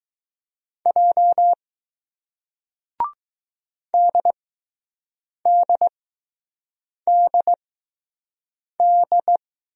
Morse code, heard as a keyed tone, about 20 wpm, spaced as for 6 wpm, 700 Hz. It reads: J DDDD